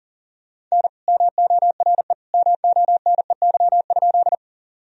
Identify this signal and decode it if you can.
Morse code, heard as a keyed tone, about 40 wpm, 700 Hz. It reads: N MORE MONEY?
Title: Sentences from Top 400 Words and Abbreviations 40wpm